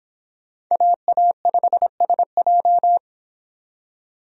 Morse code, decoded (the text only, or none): AA5SJ